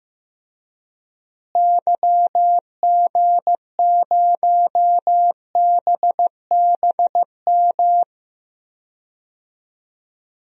Morse code, heard as a keyed tone, about 15 wpm, 700 Hz. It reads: YG0BBM
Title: DXCC-100 Call Signs - 6 Characters - Repeated 3x - 15wpm